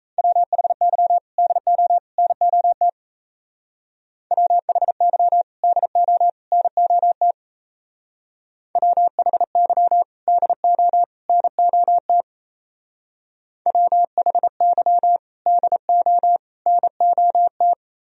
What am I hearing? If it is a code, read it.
Morse code, heard as a keyed tone, about 35 wpm, 700 Hz. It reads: WHY DO NOT WHY DO NOT WHY DO NOT WHY DO NOT
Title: Speed-Racing - Sets of 3 Words 28wpm